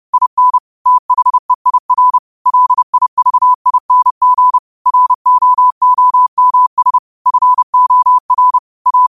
Morse code, heard as a keyed tone, about 30 wpm, 1000 Hz. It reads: IN THEIR LIVING ROOMS FOR A
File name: 30wpm